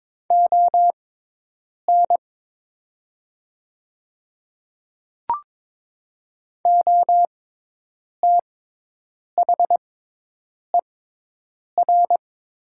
Morse code, heard as a keyed tone, about 22 wpm, spaced as for 8 wpm, 700 Hz. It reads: ON OTHER
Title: 6x Character Spacing - Top 100 Words - 22wpm